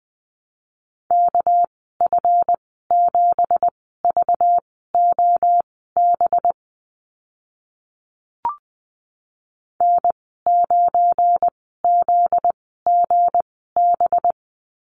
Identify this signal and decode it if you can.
Morse code, heard as a keyed tone, about 20 wpm, spaced as for 15 wpm, 700 Hz. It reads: KF7VOB N9ZGB
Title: 245 Lesson 39g - Letter z - Call Signs 20wpm